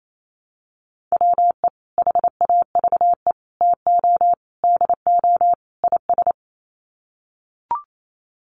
Morse code, heard as a keyed tone, about 28 wpm, 700 Hz. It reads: WE HAVE TO DO IS
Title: Sets of 5 Words Encore 28wpm